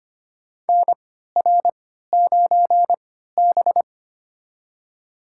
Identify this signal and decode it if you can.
Morse code, heard as a keyed tone, about 25 wpm, spaced as for 14 wpm, 700 Hz. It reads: NR9B
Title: USA Call Signs - 4 Characters - 3x Character Spacing - 25wpm